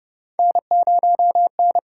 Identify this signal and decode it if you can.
Morse code, heard as a keyed tone, about 29 wpm, 700 Hz. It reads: N0N
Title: USA Call Signs - 5 Characters - 8x Character Spacing - 30wpm